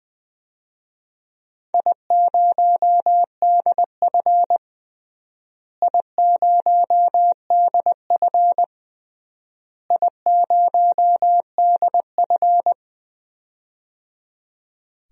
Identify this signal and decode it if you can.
Morse code, heard as a keyed tone, about 20 wpm, 700 Hz. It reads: I0DF I0DF I0DF
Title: DXCC-100 Call Signs - 4 Characters - Repeated 3x - 20wpm